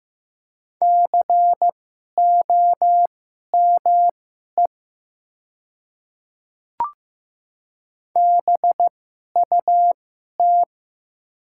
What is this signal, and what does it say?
Morse code, heard as a keyed tone, about 15 wpm, spaced as for 11 wpm, 700 Hz. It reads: COME BUT
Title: 2x Character Spacing - Top 100 Words - 15wpm